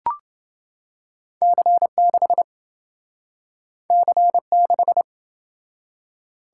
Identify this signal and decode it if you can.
Morse code, heard as a keyed tone, about 28 wpm, 700 Hz. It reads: C6 C6